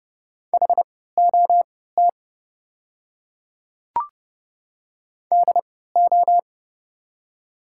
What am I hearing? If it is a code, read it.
Morse code, heard as a keyed tone, about 31 wpm, spaced as for 17 wpm, 700 Hz. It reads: HOT DO
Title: 3x Character Spacing - Top 100 Words - 30wpm